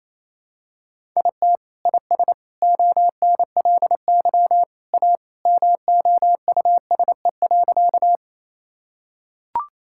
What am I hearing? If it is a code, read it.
Morse code, heard as a keyed tone, about 28 wpm, 700 Hz. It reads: IT IS ONLY A MOUSE.
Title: Sentences from Top 1000 Words - Review Part 5 - 28wpm